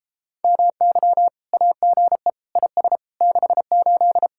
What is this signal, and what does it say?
Morse code, heard as a keyed tone, about 33 wpm, 700 Hz. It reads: MY AGE IS 68